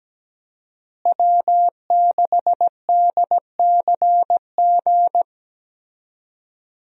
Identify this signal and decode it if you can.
Morse code, heard as a keyed tone, about 17 wpm, 700 Hz. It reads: W6DCG
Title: USA Call Signs - 5 Characters - Repeated 2x - 17wpm